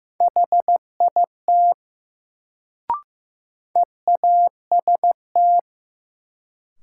Morse code, heard as a keyed tone, about 16 wpm, 700 Hz. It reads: HIT EAST
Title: Sentences from Top 1200 Words - New Word Drill Alt - 15wpm